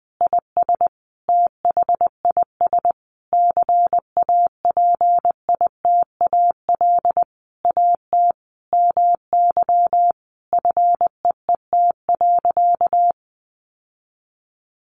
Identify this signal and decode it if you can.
Morse code, heard as a keyed tone, about 20 wpm, 700 Hz. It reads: IS THIS CAPITAL AT MY FEET.
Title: Sentences from Top 1000 Words 20wpm